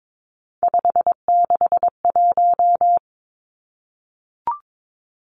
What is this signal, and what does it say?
Morse code, heard as a keyed tone, about 22 wpm, 700 Hz. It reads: 561
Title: Numbers - 3-Digit 22wpm